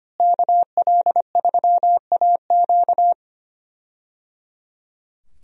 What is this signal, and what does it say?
Morse code, heard as a keyed tone, about 25 wpm, 700 Hz. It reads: KL3AQ
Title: USA Call Signs - 5 Characters - Repeated 3x - 25wpm